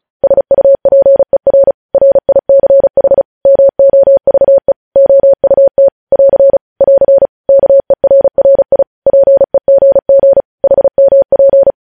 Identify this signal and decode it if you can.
Morse code, heard as a keyed tone, about 36 wpm, 550 Hz. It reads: SUPER RICH MOVE OUT <AR> <AR> KERRI PEGG HMP
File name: podcast / News headlines-1x-35x35